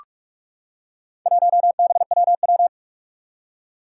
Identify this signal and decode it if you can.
Morse code, heard as a keyed tone, about 44 wpm, 700 Hz. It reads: 1BWW
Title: Four Challenging Letter-Number Combinations 45wpm